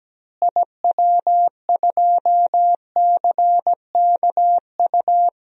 Morse code, heard as a keyed tone, about 17 wpm, 700 Hz. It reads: IW2CKU
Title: DXCC-100 Call Signs - 6 Characters - Repeated 2x - 17wpm